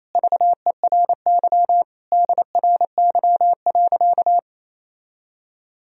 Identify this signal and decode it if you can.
Morse code, heard as a keyed tone, about 28 wpm, 700 Hz. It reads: VERY DRY.
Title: Sentences from Top 1000 Words - Review Part 4 - 28wpm